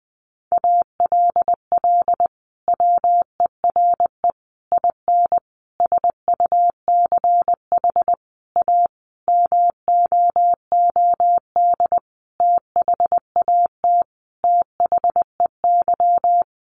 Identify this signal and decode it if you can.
Morse code, heard as a keyed tone, about 20 wpm, 700 Hz. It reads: ALL WERE IN SUCH A MOOD THAT THEY
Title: Sentences from Top 1100 Words 20wpm